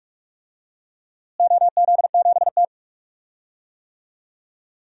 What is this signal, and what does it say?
Morse code, heard as a keyed tone, about 41 wpm, 700 Hz. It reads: OZ7T
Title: DXCC-100 Call Signs - 4 Characters - Repeated 2x - 45wpm